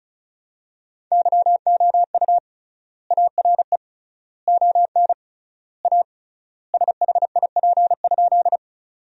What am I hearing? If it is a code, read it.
Morse code, heard as a keyed tone, about 36 wpm, 700 Hz. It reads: YOU ARE ON A SHIP?
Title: 3x Word-Spacing - Sentences from Top 400 Words - 35wpm